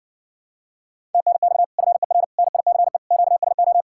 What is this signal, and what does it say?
Morse code, heard as a keyed tone, about 59 wpm, 700 Hz. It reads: TNX FER NICE QSO